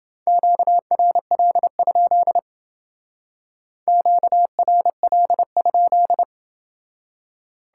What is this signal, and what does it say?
Morse code, heard as a keyed tone, about 29 wpm, 700 Hz. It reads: QRL? QRL?